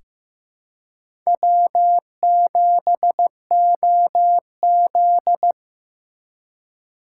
Morse code, heard as a keyed tone, about 15 wpm, 700 Hz. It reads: W7OZ
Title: USA Call Signs - 4 Characters - 4x Character Spacing - 15wpm